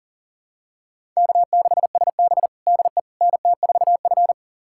Morse code, heard as a keyed tone, about 40 wpm, 700 Hz. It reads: K6SB DE NT4F
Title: USA Callsign DE Callsign - 4 Characters - 40wpm